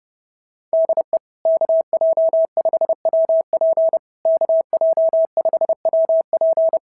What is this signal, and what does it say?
Morse code, heard as a keyed tone, about 30 wpm, 690 Hz. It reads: DE KJ5WP KJ5WP
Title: SOTA QSO Practice 30wpm